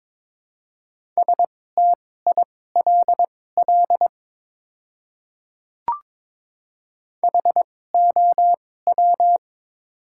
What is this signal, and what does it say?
Morse code, heard as a keyed tone, about 22 wpm, spaced as for 16 wpm, 700 Hz. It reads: STILL HOW
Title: 2x Character Spacing - Top 500 Words - 22wpm